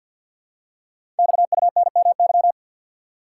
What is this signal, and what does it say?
Morse code, heard as a keyed tone, about 47 wpm, 700 Hz. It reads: XUNMY
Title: Code Groups - 5 Letters - 50wpm